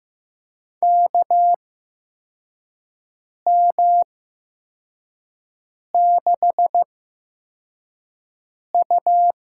Morse code, heard as a keyed tone, about 15 wpm, spaced as for 4 wpm, 700 Hz. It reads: KM6U
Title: USA Call Signs - 6 Characters - 8x Character Spacing - 15wpm